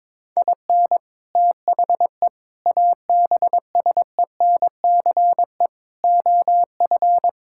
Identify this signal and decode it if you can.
Morse code, heard as a keyed tone, about 22 wpm, 700 Hz. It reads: IN THE ABSENCE OF
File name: practice / 22wpm / Sentences with 4-Word Phrases - 22wpm